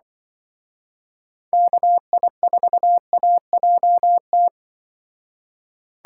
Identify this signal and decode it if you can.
Morse code, heard as a keyed tone, about 24 wpm, 700 Hz. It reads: KI4AJT
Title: Speed Racing - USA Call Signs 20wpm